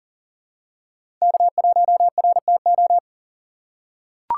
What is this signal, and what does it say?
Morse code, heard as a keyed tone, about 38 wpm, 700 Hz. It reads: K1RTO
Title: USA Call Signs - 5 Characters - 3x Character Spacing - 40wpm